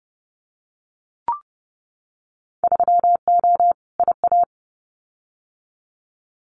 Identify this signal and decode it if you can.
Morse code, heard as a keyed tone, about 30 wpm, 700 Hz. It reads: 3O IA
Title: Winter Field Day Exchanges 30wpm